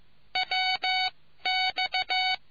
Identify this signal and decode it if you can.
Morse code, heard as a keyed tone, about 15 wpm, spaced as for 12 wpm, 750 Hz. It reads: WX